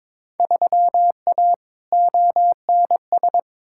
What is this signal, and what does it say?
Morse code, heard as a keyed tone, about 22 wpm, 700 Hz. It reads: 3A ONS
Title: Field Day Exchanges - 22wpm